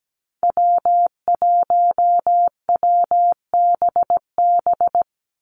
Morse code, heard as a keyed tone, about 17 wpm, 700 Hz. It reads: W1WBB
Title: Call Signs Repeated 2x - 17wpm